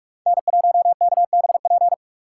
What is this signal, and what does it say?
Morse code, heard as a keyed tone, about 46 wpm, 700 Hz. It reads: N1KBP
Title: USA Call Signs - 5 Characters - 2x Character Spacing - 45wpm